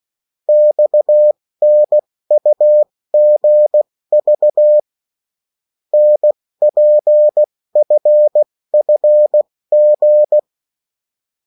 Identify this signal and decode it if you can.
Morse code, heard as a keyed tone, about 16 wpm, spaced as for 14 wpm, 600 Hz. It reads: XNUGV NPFFG
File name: morse mp3 / revision9-vitesse 14 mots